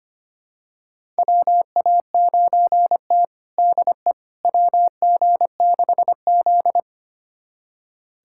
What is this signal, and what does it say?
Morse code, heard as a keyed tone, about 25 wpm, 700 Hz. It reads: WA9T DE WG6Z
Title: USA Callsign DE Callsign - 4 Characters - 25wpm